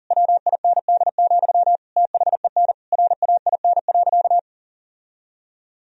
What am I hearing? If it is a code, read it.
Morse code, heard as a keyed tone, about 37 wpm, 700 Hz. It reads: WIND, THEN RAIN.
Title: Sentences from Top 1000 Words - Review Part 2 - 40wpm